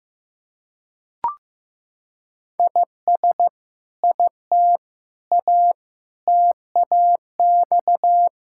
Morse code, heard as a keyed tone, about 15 wpm, 700 Hz. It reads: IS IT A TAX
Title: Sentences from Top 1000 Words - Review Part 5 - 15wpm